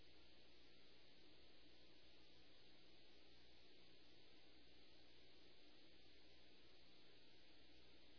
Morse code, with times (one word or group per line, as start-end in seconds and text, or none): none